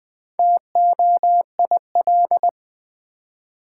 Morse code, heard as a keyed tone, about 20 wpm, 700 Hz. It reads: TOIL
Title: Build A Word - 2 Letters At A Time - 6-Letter Words - 20wpm